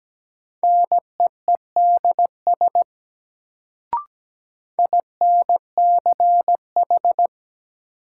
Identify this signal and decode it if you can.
Morse code, heard as a keyed tone, about 17 wpm, 700 Hz. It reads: NEEDS INCH
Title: Sentences from Top 1000 Words - Review Part 3 - New Word Drill - 17wpm